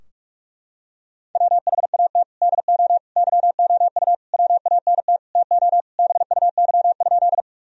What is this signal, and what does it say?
Morse code, heard as a keyed tone, about 45 wpm, 700 Hz. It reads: WHAT DO YOU WANT TO BUY?
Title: Sentences from Top 1000 Words - Review Part 4 - 45wpm